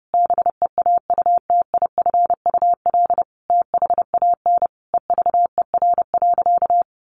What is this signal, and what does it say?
Morse code, heard as a keyed tone, about 30 wpm, 700 Hz. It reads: BEAUTIFUL THAN EVER.
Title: Sentences from Top 1200 Words 30wpm